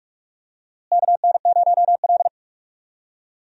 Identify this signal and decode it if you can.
Morse code, heard as a keyed tone, about 46 wpm, 700 Hz. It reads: KN0L
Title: USA Call Signs - 4 Characters - Repeated 3x - 45wpm